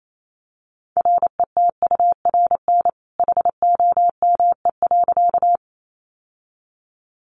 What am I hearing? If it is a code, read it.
Morse code, heard as a keyed tone, about 28 wpm, 700 Hz. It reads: RETURN HOME.
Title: Sentences from Top 800 Words 28wpm